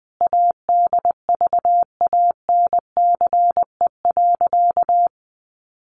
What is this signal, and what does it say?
Morse code, heard as a keyed tone, about 20 wpm, 700 Hz. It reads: ADVANCE.
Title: Sentences from Top 1400 Words 20wpm